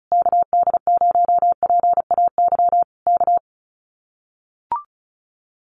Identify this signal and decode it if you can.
Morse code, heard as a keyed tone, about 34 wpm, 700 Hz. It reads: KD0PAY K
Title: Common QSO Phrases Encore 35wpm